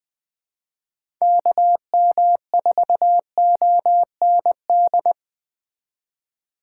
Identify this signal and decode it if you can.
Morse code, heard as a keyed tone, about 20 wpm, 700 Hz. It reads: KM4OND
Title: USA Call Signs - 6 Characters - 2x Character Spacing - 20wpm